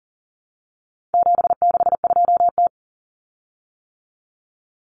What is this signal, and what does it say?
Morse code, heard as a keyed tone, about 40 wpm, 700 Hz. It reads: 762T
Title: Common Cut Numbers - 4-Digit 40wpm